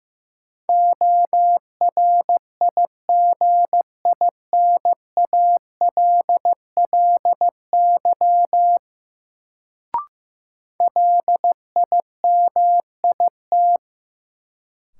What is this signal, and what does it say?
Morse code, heard as a keyed tone, about 15 wpm, 700 Hz. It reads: ORIGINALLY LIMIT
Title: Sentences from Top 1500 Words - New Word Drill - 15wpm